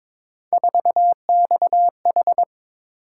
Morse code, heard as a keyed tone, about 23 wpm, 700 Hz. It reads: 4XH
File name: practice / 22wpm / Three Letter-Number Combinations - 22wpm